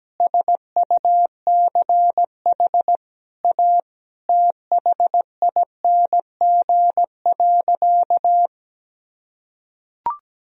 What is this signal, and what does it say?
Morse code, heard as a keyed tone, about 17 wpm, 700 Hz. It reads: SUCH A THING.
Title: Sentences from Top 200 Words and Abbreviations 17wpm